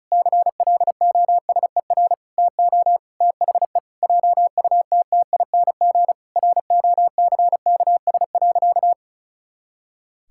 Morse code, heard as a keyed tone, about 34 wpm, 700 Hz. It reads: CLOSER TO THE JUTTING ROCKS.
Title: Pangram Challenge v2 - 35wpm